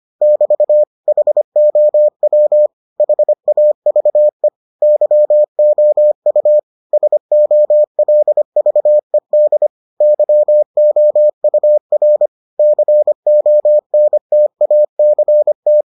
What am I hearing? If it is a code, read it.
Morse code, heard as a keyed tone, about 25 wpm, 600 Hz. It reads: <BT> HOW HAVE YOU SOLVED YOUR CONTACT